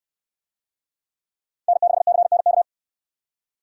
Morse code, heard as a keyed tone, about 63 wpm, 700 Hz. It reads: N6CNK